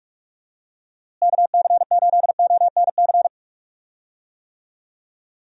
Morse code, heard as a keyed tone, about 46 wpm, 700 Hz. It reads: KC8ONC
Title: USA Call Signs - 6 Characters - Repeated 3x - 45wpm